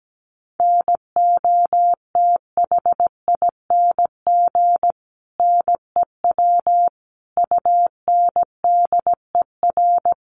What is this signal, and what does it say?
Morse code, heard as a keyed tone, about 17 wpm, 700 Hz. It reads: NOTHING NEW UNDER
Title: Sentences from Top 500 Words 17wpm